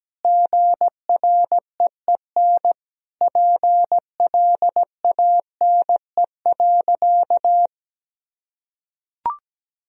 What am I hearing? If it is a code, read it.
Morse code, heard as a keyed tone, about 17 wpm, 700 Hz. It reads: GREEN PLANE.
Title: Sentences from Top 1000 Words - Review Part 3 - 17wpm